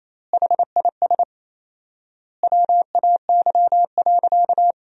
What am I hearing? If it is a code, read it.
Morse code, heard as a keyed tone, about 28 wpm, 700 Hz. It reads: HIS WAY.
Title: 4x Word-Spacing - Sentences from Top 300 Words - 28wpm